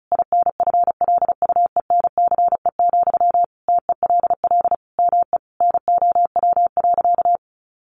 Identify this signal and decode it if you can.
Morse code, heard as a keyed tone, about 34 wpm, 700 Hz. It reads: INFLUENCE, TELL ME NOW.
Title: Sentences from Top 1100 Words 35wpm